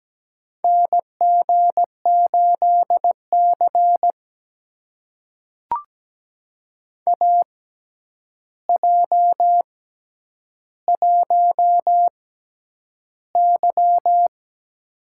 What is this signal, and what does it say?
Morse code, heard as a keyed tone, about 17 wpm, spaced as for 9 wpm, 700 Hz. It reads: NG8C AJ1Y
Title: USA Call Signs - 4 Characters - 6x Character Spacing - 17wpm